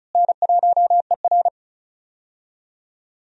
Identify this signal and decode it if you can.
Morse code, heard as a keyed tone, about 34 wpm, 700 Hz. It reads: N1ER
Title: USA Call Signs - 4 Characters - Repeated 3x - 35wpm